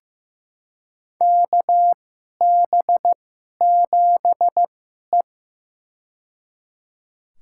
Morse code, heard as a keyed tone, about 15 wpm, spaced as for 11 wpm, 700 Hz. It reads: KB7E
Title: USA Call Signs - 2x Character Spacing - 15wpm